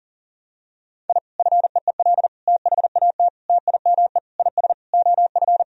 Morse code, heard as a keyed tone, about 40 wpm, 700 Hz. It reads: I FEEL THAT TIME IS OF
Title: Sentences from Top 1000 Words - Review Part 3 - 40wpm